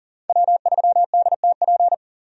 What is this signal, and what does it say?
Morse code, heard as a keyed tone, about 40 wpm, 700 Hz. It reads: W3DTP
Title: USA Call Signs - 5 Characters - 40wpm